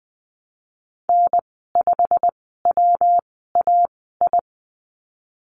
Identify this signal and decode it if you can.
Morse code, heard as a keyed tone, about 20 wpm, spaced as for 15 wpm, 700 Hz. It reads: N5WAI
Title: 186 Lesson 30g - Letter b - Call Signs 20wpm